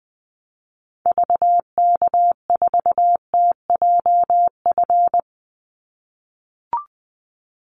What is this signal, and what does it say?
Morse code, heard as a keyed tone, about 20 wpm, 700 Hz. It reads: VK4TJF
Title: Call Signs - 20wpm